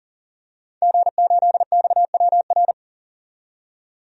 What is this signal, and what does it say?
Morse code, heard as a keyed tone, about 40 wpm, 700 Hz. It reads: G8XWR
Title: DXCC-100 Call Signs - 5 Characters - 40wpm